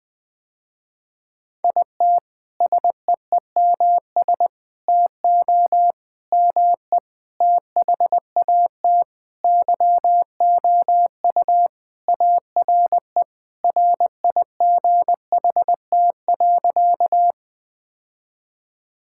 Morse code, heard as a keyed tone, about 20 wpm, 700 Hz. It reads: IT SEEMS TO ME THAT YOU ARE RIGHT.
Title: Sentences with 5-Word Phrases - Spoken Beginning - 20wpm